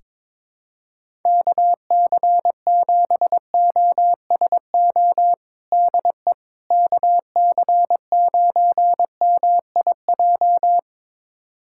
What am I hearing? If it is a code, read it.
Morse code, heard as a keyed tone, about 22 wpm, 700 Hz. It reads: KC7OSO DE KC9MIJ
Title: USA Callsign DE Callsign - 6 Characters - 22wpm